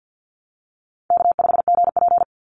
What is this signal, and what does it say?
Morse code, heard as a keyed tone, about 50 wpm, 700 Hz. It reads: K5GP
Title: Easy Call Signs Repeated 3x - 50wpm